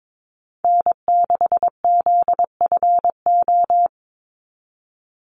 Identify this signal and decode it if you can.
Morse code, heard as a keyed tone, about 22 wpm, 700 Hz. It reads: N6ZFO
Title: Call Signs - 22wpm